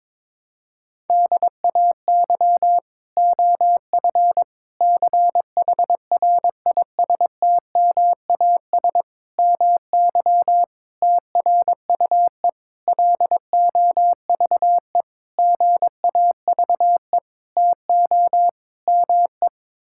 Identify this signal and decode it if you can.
Morse code, heard as a keyed tone, about 22 wpm, 700 Hz. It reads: DAY OF CHRISTMAS MY TRUE LOVE GAVE TO ME
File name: Sets of 9 Words - 22wpm